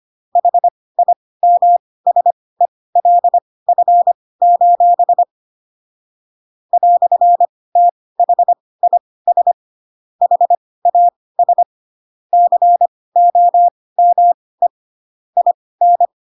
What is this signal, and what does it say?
Morse code, heard as a keyed tone, about 25 wpm, spaced as for 18 wpm, 700 Hz. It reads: HIMSELF: "THIS HAS COME IN
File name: Pinocchio / Pinocchio ch1 12 25 ik1hge